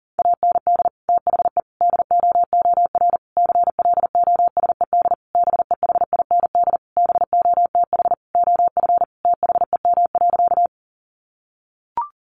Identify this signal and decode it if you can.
Morse code, heard as a keyed tone, about 40 wpm, 700 Hz. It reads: AND THE DOOR CLOSED BEHIND BOTH OF THEM.
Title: Sentences from Top 1400 Words 40wpm